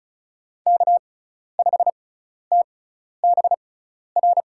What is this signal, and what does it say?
Morse code, heard as a keyed tone, about 34 wpm, spaced as for 12 wpm, 700 Hz. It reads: K5TBR
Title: USA Call Signs - 6x Character Spacing - 35wpm